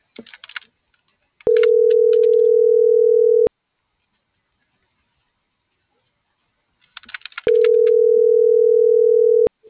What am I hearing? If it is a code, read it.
Ringback tone (North America), 440 + 480 Hz.